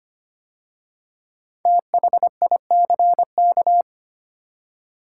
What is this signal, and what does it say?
Morse code, heard as a keyed tone, about 25 wpm, 700 Hz. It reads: THICK
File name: 5 Letter Words - First 3 Letters Given - 25wpm